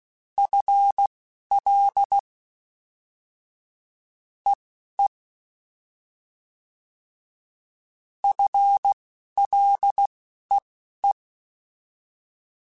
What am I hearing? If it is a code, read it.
Morse code, heard as a keyed tone, about 16 wpm, spaced as for 11 wpm, 800 Hz. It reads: FL EE FLEE
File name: FL / FL mp3s / Fast FL @10